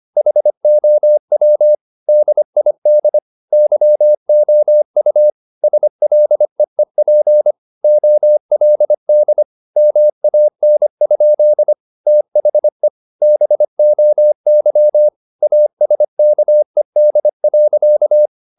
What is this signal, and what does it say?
Morse code, heard as a keyed tone, about 25 wpm, 600 Hz. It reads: HOW DID YOU SLEEP OLD MAN? THE BOY ASKED.
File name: the old man and the sea part 1 25wpm